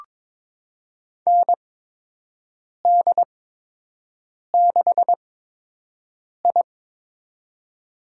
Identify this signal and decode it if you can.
Morse code, heard as a keyed tone, about 22 wpm, spaced as for 6 wpm, 700 Hz. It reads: ND6I
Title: USA Call Signs - 8x Character Spacing - 22wpm